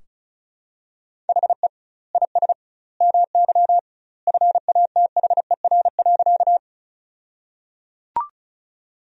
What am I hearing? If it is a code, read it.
Morse code, heard as a keyed tone, about 34 wpm, 700 Hz. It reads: HE IS MY FATHER.
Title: 2x Word-Spacing - Sentences from Top 300 Words - 35wpm